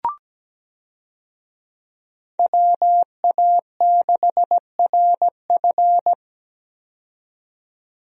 Morse code, heard as a keyed tone, about 17 wpm, 700 Hz. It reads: WA6RF
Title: USA Call Signs - 5 Characters - 17wpm